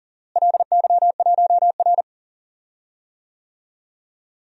Morse code, heard as a keyed tone, about 40 wpm, 700 Hz. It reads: LY1R
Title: DXCC-100 Call Signs - 4 Characters - Repeated 2x - 40wpm